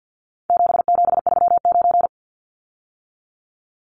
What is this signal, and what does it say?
Morse code, heard as a keyed tone, about 50 wpm, 700 Hz. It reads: 7739
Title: Numbers - 4-Digit 50wpm